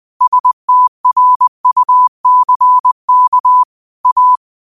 Morse code, heard as a keyed tone, about 20 wpm, 1000 Hz. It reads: STRUCK A
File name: 20wpm